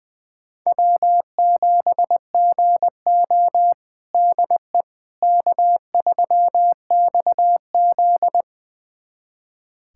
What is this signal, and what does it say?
Morse code, heard as a keyed tone, about 21 wpm, 700 Hz. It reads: W7GO DE K3XZ
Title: USA Callsign DE Callsign - 4 Characters - 20wpm